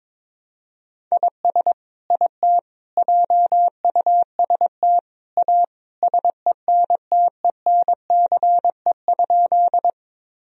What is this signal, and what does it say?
Morse code, heard as a keyed tone, about 22 wpm, 700 Hz. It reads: IS IT JUST A SENTENCE?
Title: Sentences from Top 1000 Words - Review Part 1 - 22wpm